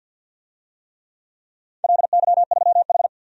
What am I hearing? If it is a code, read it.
Morse code, heard as a keyed tone, about 47 wpm, 700 Hz. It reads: LY3H
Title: DXCC-100 Call Signs - 4 Characters - 50wpm